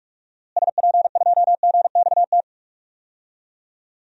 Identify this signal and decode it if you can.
Morse code, heard as a keyed tone, about 46 wpm, 700 Hz. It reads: SP2GXT
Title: DXCC-100 Call Signs - 6 Characters - Repeated 2x - 45wpm